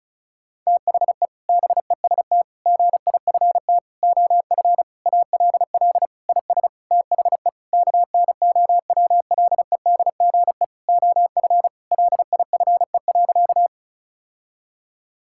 Morse code, heard as a keyed tone, about 34 wpm, 700 Hz. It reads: THE BEST GIFT OF ALL IS THE KNOWLEDGE OF LIFE.
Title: Sentences from Top 1000 Words - Review Part 6 - 35wpm